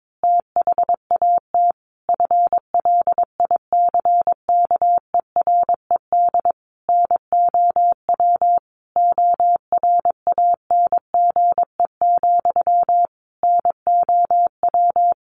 Morse code, heard as a keyed tone, about 22 wpm, 700 Hz. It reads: THAT FLICKERED NOW ORANGE, NOW
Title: A Sound Of Thunder 22wpm